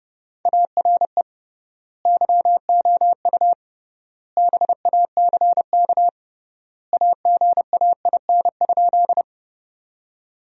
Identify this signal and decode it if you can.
Morse code, heard as a keyed tone, about 32 wpm, 700 Hz. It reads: ARE YOU BACK AGAIN?
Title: 3x Word-Spacing - Sentences from Top 200 Words - 30wpm